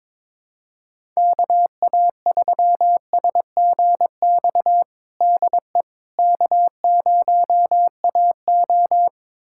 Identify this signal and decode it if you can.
Morse code, heard as a keyed tone, about 22 wpm, 700 Hz. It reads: KA3SGX DE K0AO
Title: USA Callsign DE Callsign 22wpm